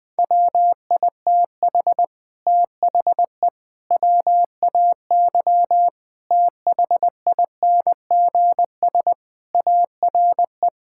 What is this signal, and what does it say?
Morse code, heard as a keyed tone, about 19 wpm, 700 Hz. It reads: WITH THE WAY THINGS ARE
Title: Sentences with 8-Word Phrases - 20wpm